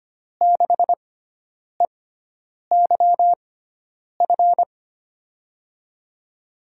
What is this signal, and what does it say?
Morse code, heard as a keyed tone, about 25 wpm, spaced as for 9 wpm, 700 Hz. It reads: 6EYF
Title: USA Call Signs - 6x Character Spacing - 25wpm